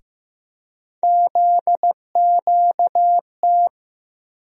Morse code, heard as a keyed tone, about 15 wpm, 700 Hz. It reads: ZQT
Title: USA Call Sign Postfixes - 3 Letters 15wpm